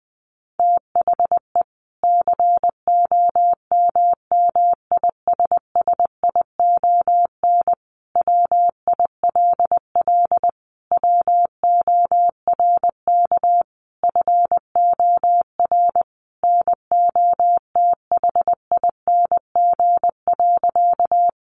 Morse code, heard as a keyed tone, about 19 wpm, 700 Hz. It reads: THE COMMISSION WILL WORK FOR NOTHING.
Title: Sentences from Top 1400 Words 20wpm